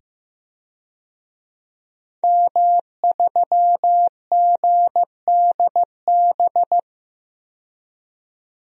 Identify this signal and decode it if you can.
Morse code, heard as a keyed tone, about 15 wpm, 700 Hz. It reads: M3GDB
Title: DXCC-100 Call Signs - 5 Characters - 15wpm